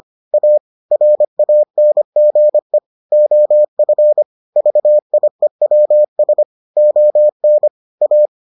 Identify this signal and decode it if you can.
Morse code, heard as a keyed tone, about 25 wpm, 600 Hz. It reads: A RANGE OF VIEWS ON A